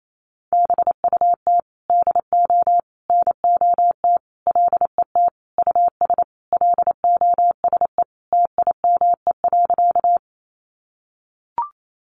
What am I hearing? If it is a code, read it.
Morse code, heard as a keyed tone, about 28 wpm, 700 Hz. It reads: BUT DO NOT LET US LOSE TIME.
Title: Sentences from Top 1100 Words 28wpm